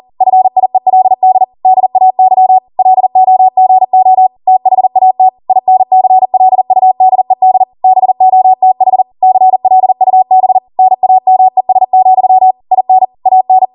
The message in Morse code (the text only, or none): FIELD DAY LOGO THAT INCLUDED BOTH CLUB NAMES, IN AN